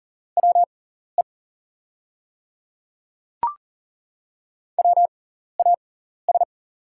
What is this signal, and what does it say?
Morse code, heard as a keyed tone, about 40 wpm, spaced as for 14 wpm, 690 Hz. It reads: WE WAS
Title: 6x Character Spacing - Top 100 Words - 40wpm